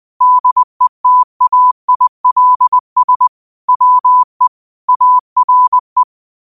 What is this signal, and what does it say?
Morse code, heard as a keyed tone, about 20 wpm, 1000 Hz. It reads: DETAILS WE ARE
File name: qotdfiles / 20wpm